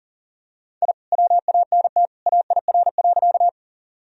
Morse code, heard as a keyed tone, about 39 wpm, 700 Hz. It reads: I WANT AIR.
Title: Sentences from Top 1000 Words - Review Part 1 - 40wpm